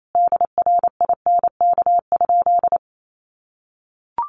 Morse code, heard as a keyed tone, about 28 wpm, 700 Hz. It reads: DRINK?
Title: Sentences from Top 1500 Words 28wpm